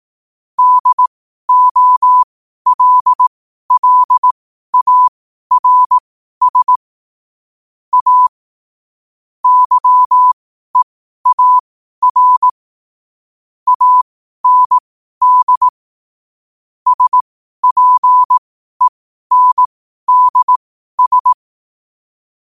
Morse code, heard as a keyed tone, about 18 wpm, spaced as for 12 wpm, 1000 Hz. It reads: DOLLARS A YEAR AND SPENDS
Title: 12 WPM morse code quotes for Fri, 01 Mar 2024